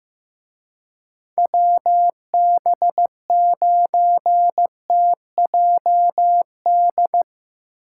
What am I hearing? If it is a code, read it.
Morse code, heard as a keyed tone, about 16 wpm, 700 Hz. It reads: WB9TJD
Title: Speed Racing - USA Call Signs 15wpm